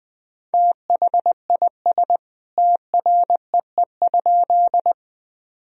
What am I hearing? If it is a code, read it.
Morse code, heard as a keyed tone, about 20 wpm, 700 Hz. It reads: THIS TREE?
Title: Sentences from Top 1000 Words - Review Part 1 - 20wpm